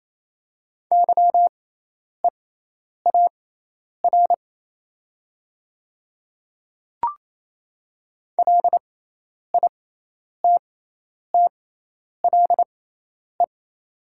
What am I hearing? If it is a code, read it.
Morse code, heard as a keyed tone, about 28 wpm, spaced as for 10 wpm, 700 Hz. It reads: YEAR LITTLE
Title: 6x Character Spacing - Top 300 Words - 28wpm